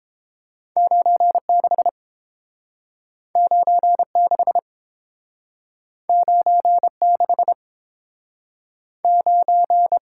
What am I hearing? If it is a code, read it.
Morse code, heard as a keyed tone, about 29 wpm, 700 Hz. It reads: 96 96 96 9